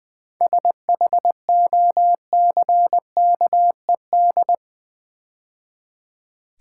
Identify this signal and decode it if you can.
Morse code, heard as a keyed tone, about 20 wpm, 690 Hz. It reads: SHOCKED
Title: 7 Letter Words - First 3 Letters Given - 20wpm